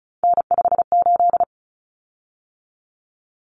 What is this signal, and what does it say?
Morse code, heard as a keyed tone, about 34 wpm, 700 Hz. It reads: N58
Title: Common Cut Numbers - 3-Digit 35wpm